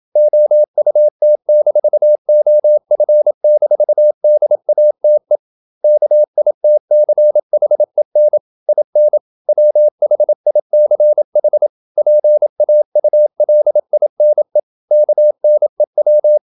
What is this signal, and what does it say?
Morse code, heard as a keyed tone, about 27 wpm, 600 Hz. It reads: OUT-OF-DATE KITCHEN IN WHICH PAULINE KNEW
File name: Never 0027wpm